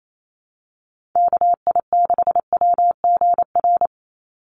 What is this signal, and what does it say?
Morse code, heard as a keyed tone, about 28 wpm, 700 Hz. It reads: KI6WGR
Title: USA Call Signs 28wpm